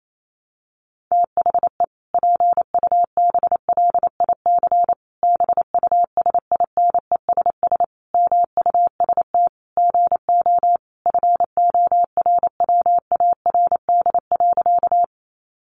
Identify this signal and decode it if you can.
Morse code, heard as a keyed tone, about 28 wpm, 700 Hz. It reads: THE PUBLIC BUSINESS MUST GO FORWARD.
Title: Sentences from Top 1300 Words 28wpm